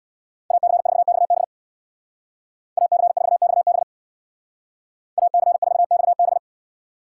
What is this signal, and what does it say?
Morse code, heard as a keyed tone, about 64 wpm, 700 Hz. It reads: AC4XB AC4XB AC4XB